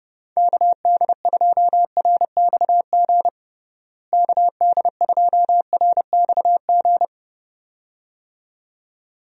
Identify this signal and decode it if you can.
Morse code, heard as a keyed tone, about 30 wpm, 700 Hz. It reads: KD2RXG KD2RXG
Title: USA Call Signs - 6 Characters - Repeated 2x - 30wpm